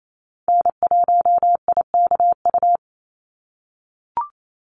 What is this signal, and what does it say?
Morse code, heard as a keyed tone, about 28 wpm, 700 Hz. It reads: N1IKU
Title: USA Call Signs 28wpm